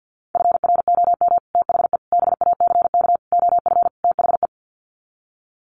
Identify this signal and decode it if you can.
Morse code, heard as a keyed tone, about 50 wpm, 700 Hz. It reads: FROM THE BACK OF THE
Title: Sets of 5 Words Encore 50wpm